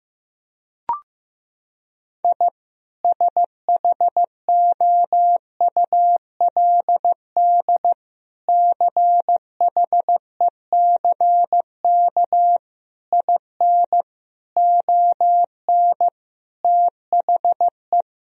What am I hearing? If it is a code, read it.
Morse code, heard as a keyed tone, about 15 wpm, 700 Hz. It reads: I SHOULD CHECK IN ON THE
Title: Sentences from Top 1000 Words - Review Part 3 - 15wpm